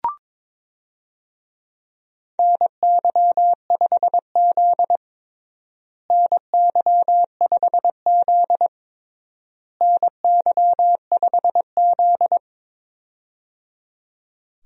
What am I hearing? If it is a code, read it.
Morse code, heard as a keyed tone, about 22 wpm, 700 Hz. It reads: NY5Z NY5Z NY5Z
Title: USA Call Signs - 4 Characters - Repeated 3x - 22wpm